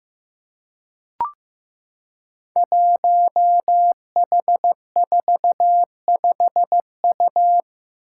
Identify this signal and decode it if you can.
Morse code, heard as a keyed tone, about 15 wpm, 700 Hz. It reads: 1H45U
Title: Five Challenging Letter-Number Combinations 15wpm